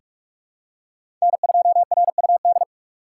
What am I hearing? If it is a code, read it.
Morse code, heard as a keyed tone, about 44 wpm, 700 Hz. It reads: N2RUD
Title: USA Call Signs - 5 Characters - 2x Character Spacing - 45wpm